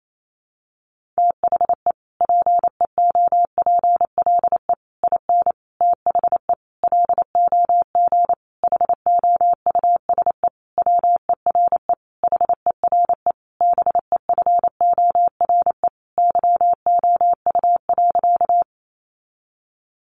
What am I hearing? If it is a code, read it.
Morse code, heard as a keyed tone, about 28 wpm, 700 Hz. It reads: THE PEOPLE IN THE LOG HOUSE WERE HERE BEFORE YOU.
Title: Sentences from Top 1400 Words 28wpm